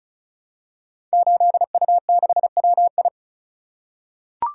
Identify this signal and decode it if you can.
Morse code, heard as a keyed tone, about 36 wpm, 690 Hz. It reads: 8U6WI
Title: Code Groups - 5 Letter:Numbers - 35wpm